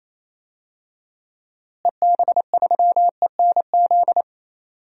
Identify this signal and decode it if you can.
Morse code, heard as a keyed tone, about 27 wpm, 700 Hz. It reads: EB3ENZ
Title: DXCC-100 Call Signs - 6 Characters - Repeated 2x - 28wpm